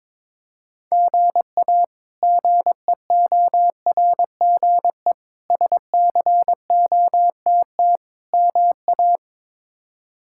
Morse code, heard as a keyed tone, about 22 wpm, 700 Hz. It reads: GA GEORGE SCOTT MA
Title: K1USN SST QSO Practice - 22wpm